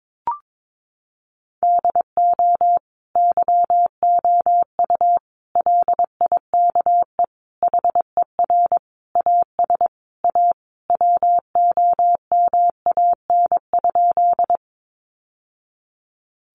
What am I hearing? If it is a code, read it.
Morse code, heard as a keyed tone, about 22 wpm, 700 Hz. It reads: DO YOU LIKE HER AS A WOMAN?
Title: Sentences from Top 1100 Words 22wpm